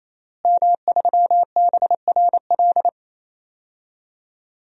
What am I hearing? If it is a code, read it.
Morse code, heard as a keyed tone, about 28 wpm, 700 Hz. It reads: M3BRL
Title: DXCC-100 Call Signs - 5 Characters - Repeated 3x - 28wpm